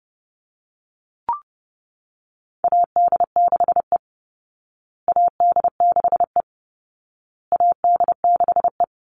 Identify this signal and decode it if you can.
Morse code, heard as a keyed tone, about 30 wpm, 700 Hz. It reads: AD6E AD6E AD6E
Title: Call Signs Repeated 2x - 30wpm